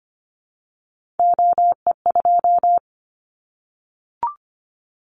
Morse code, heard as a keyed tone, about 25 wpm, 700 Hz. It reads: OE2
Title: Callsign Prefixes for Top 100 DXCC - 25wpm